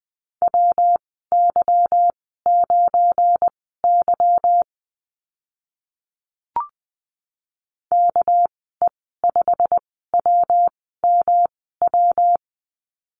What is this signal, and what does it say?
Morse code, heard as a keyed tone, about 20 wpm, spaced as for 14 wpm, 700 Hz. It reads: WY9Y KE5WMW
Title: 213 Lesson 34g - Number 8 - Call Signs 20wpm